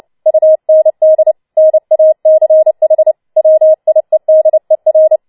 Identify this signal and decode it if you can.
Morse code, heard as a keyed tone, about 29 wpm, 630 Hz. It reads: UND NACH WIEDER